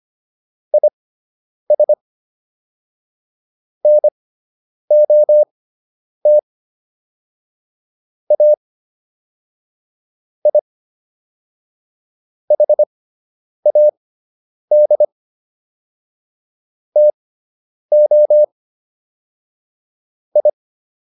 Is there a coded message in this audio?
Yes: Morse code from 0.7 to 20.5 s, heard as a keyed tone, about 25 wpm, spaced as for 9 wpm, 600 Hz. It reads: IS NOT A I HAD TO I